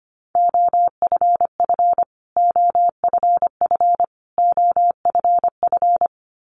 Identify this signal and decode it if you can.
Morse code, heard as a keyed tone, about 25 wpm, 700 Hz. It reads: OFF OFF OFF
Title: Top 200 Words Repeated 25wpm